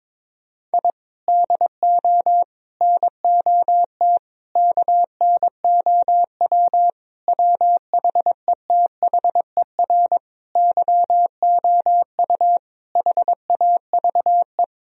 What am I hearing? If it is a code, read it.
Morse code, heard as a keyed tone, about 22 wpm, 700 Hz. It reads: I DO NOT KNOW WHETHER YOU HAVE
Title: Sentences from Top 1000 Words - Review Part 3 - 22wpm